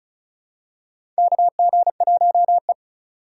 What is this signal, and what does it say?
Morse code, heard as a keyed tone, about 36 wpm, 700 Hz. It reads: KG1E
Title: USA Call Signs - 3x Character Spacing - 35wpm